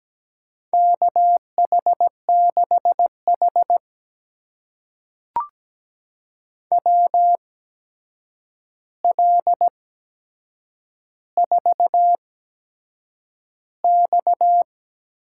Morse code, heard as a keyed tone, about 17 wpm, spaced as for 7 wpm, 700 Hz. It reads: KH6H WL4X
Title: USA Call Signs - 8x Character Spacing - 17wpm